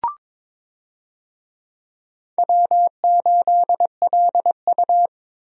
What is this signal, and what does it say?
Morse code, heard as a keyed tone, about 22 wpm, 700 Hz. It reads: W8LU
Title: USA Call Signs - 4 Characters - 22wpm